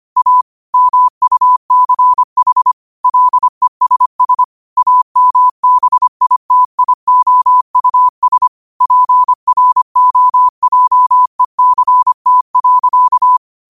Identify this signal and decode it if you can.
Morse code, heard as a keyed tone, about 25 wpm, 1000 Hz. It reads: A MUCH LESS AMBITIOUS PROJECT.
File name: qotdfiles / 25wpm